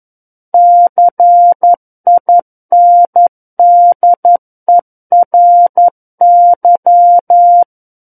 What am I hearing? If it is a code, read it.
Morse code, heard as a keyed tone, about 11 wpm, 700 Hz. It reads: CINDERY
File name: Book One / 02-Chapter 2 - 11 WPM